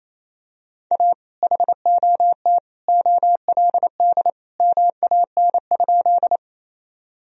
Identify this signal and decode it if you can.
Morse code, heard as a keyed tone, about 28 wpm, 700 Hz. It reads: A HOT OLD MAN?
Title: Sentences from Top 1000 Words - Review Part 3 - 28wpm